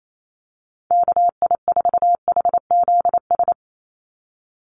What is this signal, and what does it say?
Morse code, heard as a keyed tone, about 28 wpm, 700 Hz. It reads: KI4HZS
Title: USA Call Signs 28wpm